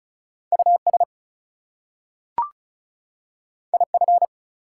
Morse code, heard as a keyed tone, about 36 wpm, 700 Hz. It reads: US IF
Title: Two Letter Words - 35wpm